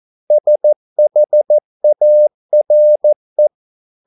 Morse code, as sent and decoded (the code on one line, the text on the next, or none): ... .... .- .-. .
SHARE